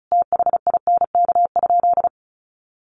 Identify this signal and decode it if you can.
Morse code, heard as a keyed tone, about 34 wpm, 700 Hz. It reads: THINK?
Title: Sentences from Top 200 Words 35wpm